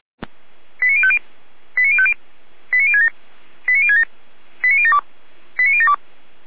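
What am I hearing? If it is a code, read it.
Five-tone selective call (ZVEI), heard as quick runs of five tones, tones of about 70 ms. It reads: 89050 89050 89067 89067 89072 89072
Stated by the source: ZVEI five-tone selective call received by SDR.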